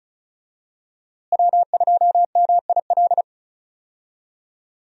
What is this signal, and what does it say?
Morse code, heard as a keyed tone, about 36 wpm, 700 Hz. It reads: W2MIL
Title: USA Call Signs - 5 Characters - Repeated 2x - 35wpm